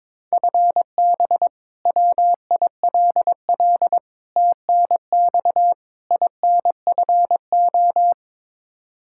Morse code, heard as a keyed tone, about 22 wpm, 700 Hz. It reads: FB WILL TNX INFO
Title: Practice QSO - Exchange 3 - More Info - 22wpm